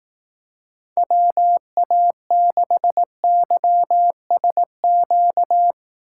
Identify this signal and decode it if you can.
Morse code, heard as a keyed tone, about 18 wpm, 700 Hz. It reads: WA6YSQ
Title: Speed Racing - USA 2x3 Call Signs 15wpm